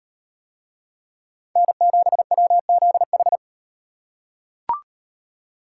Morse code, heard as a keyed tone, about 36 wpm, 700 Hz. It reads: N7WZH